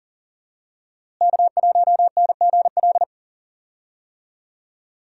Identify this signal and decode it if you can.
Morse code, heard as a keyed tone, about 40 wpm, 700 Hz. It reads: K1NGL
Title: USA Call Signs - 5 Characters - Repeated 3x - 40wpm